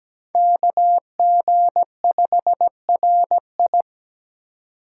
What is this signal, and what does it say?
Morse code, heard as a keyed tone, about 20 wpm, 700 Hz. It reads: KG5RI
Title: Speed Racing - USA 2x2 Call Signs 17wpm